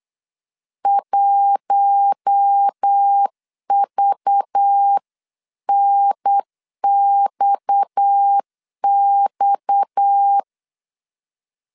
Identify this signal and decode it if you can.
Morse code, heard as a keyed tone, about 8 wpm, 800 Hz. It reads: 1VNXX
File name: LEZIONI / 6 PKX0 / 6 PKX 8wpm 3041